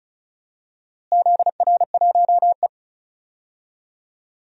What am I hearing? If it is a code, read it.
Morse code, heard as a keyed tone, about 36 wpm, 690 Hz. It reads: ZR1E
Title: DXCC-100 Call Signs - 4 Characters - Repeated 3x - 35wpm